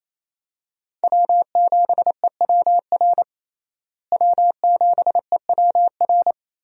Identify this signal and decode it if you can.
Morse code, heard as a keyed tone, about 28 wpm, 700 Hz. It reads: W7EWR W7EWR
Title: USA Call Signs - 5 Characters - Repeated 2x - 28wpm